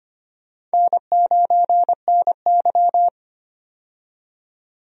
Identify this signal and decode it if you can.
Morse code, heard as a keyed tone, about 25 wpm, 700 Hz. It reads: N9NY
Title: USA Call Signs - 4 Characters - 25wpm